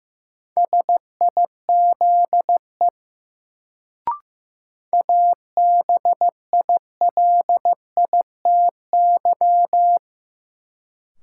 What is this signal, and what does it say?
Morse code, heard as a keyed tone, about 15 wpm, 700 Hz. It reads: SIZE ABILITY
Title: Sets of 3 Words - New Word Drill - 15wpm